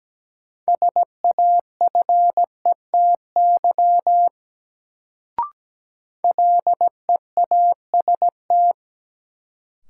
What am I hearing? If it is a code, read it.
Morse code, heard as a keyed tone, about 17 wpm, 700 Hz. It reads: SAFETY LEAST
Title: Sentences from Top 700 Words - New Word Drill - 17wpm